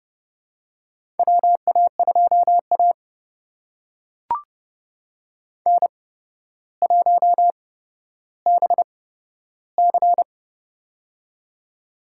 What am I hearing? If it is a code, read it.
Morse code, heard as a keyed tone, about 30 wpm, spaced as for 13 wpm, 700 Hz. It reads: WA2A N1BC
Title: USA Call Signs - 4 Characters - 8x Character Spacing - 30wpm